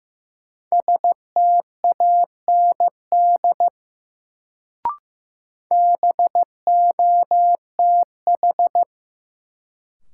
Morse code, heard as a keyed tone, about 15 wpm, 700 Hz. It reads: STAND BOTH
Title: Sentences from Top 300 Words - New Word Drill - 15wpm